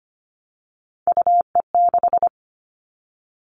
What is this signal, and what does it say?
Morse code, heard as a keyed tone, about 25 wpm, 700 Hz. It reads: UE6
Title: All Cut Numbers - 3-Digit 25wpm